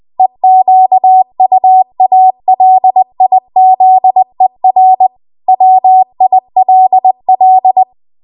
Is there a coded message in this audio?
Yes: Morse code from 0.2 to 7.8 s, heard as a keyed tone, about 20 wpm, 750 Hz. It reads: EQUALIZER WILL